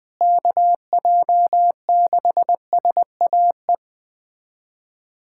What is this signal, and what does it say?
Morse code, heard as a keyed tone, about 21 wpm, 700 Hz. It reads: KJ6SAE
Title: USA Callsign DE Callsign 20wpm